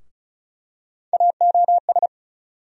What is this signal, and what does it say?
Morse code, heard as a keyed tone, about 36 wpm, 700 Hz. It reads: AOS
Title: USA Call Sign Postfixes - 3 Letters 35wpm